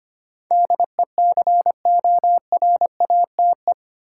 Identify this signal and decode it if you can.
Morse code, heard as a keyed tone, about 25 wpm, 700 Hz. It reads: DECORATE
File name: Eight Letter Words - 25wpm